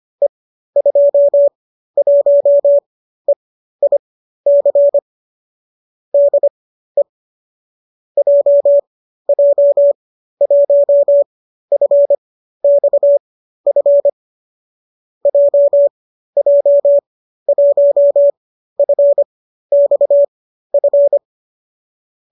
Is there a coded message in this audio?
Yes: Morse code from 0.2 to 21.2 s, heard as a keyed tone, about 25 wpm, spaced as for 13 wpm, 570 Hz. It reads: E21EIC DE JJ1FXF JJ1FXF